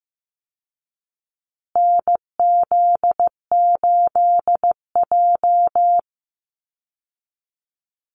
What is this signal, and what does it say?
Morse code, heard as a keyed tone, about 15 wpm, 700 Hz. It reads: NZ8J
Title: Call Signs Repeated 2x - 15wpm